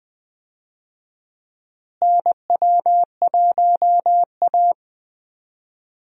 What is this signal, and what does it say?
Morse code, heard as a keyed tone, about 20 wpm, 700 Hz. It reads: NW1A
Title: USA Call Signs - 4 Characters - Repeated 3x - 20wpm